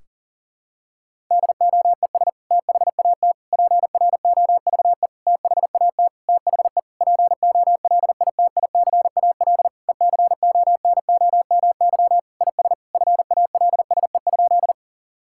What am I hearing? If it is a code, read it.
Morse code, heard as a keyed tone, about 38 wpm, 700 Hz. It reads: DOES THAT PROVE THAT THE POLITICAL ECONOMY IS FALSE?
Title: Sentences from Top 1600 Words 40wpm